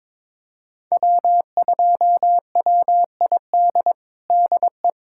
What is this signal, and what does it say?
Morse code, heard as a keyed tone, about 22 wpm, 700 Hz. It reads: W2WID DE
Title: USA Callsign DE Callsign - 5 Characters - 22wpm